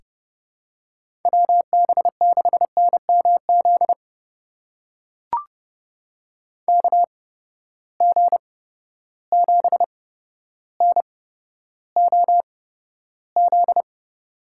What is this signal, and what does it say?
Morse code, heard as a keyed tone, about 32 wpm, spaced as for 13 wpm, 700 Hz. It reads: WB6NMZ KG7NOZ
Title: USA Call Signs - 6 Characters - 8x Character Spacing - 30wpm